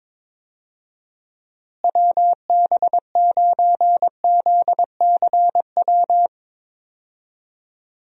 Morse code, heard as a keyed tone, about 21 wpm, 700 Hz. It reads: WB9ZCW
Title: USA Call Signs - 6 Characters - Repeated 2x - 22wpm